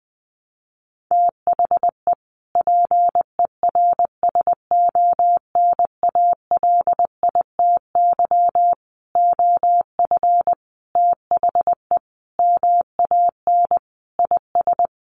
Morse code, heard as a keyed tone, about 20 wpm, 700 Hz. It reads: THE PERSONALITY OF THE MAN IS
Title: Sentences from Top 1400 Words 20wpm